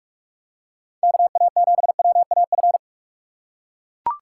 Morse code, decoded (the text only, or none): KA7WAF